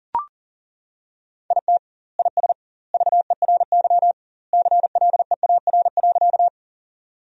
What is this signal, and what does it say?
Morse code, heard as a keyed tone, about 40 wpm, 700 Hz. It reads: IT IS VERY CLEAR.
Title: 2x Word-Spacing - Sentences from Top 500 Words - 40wpm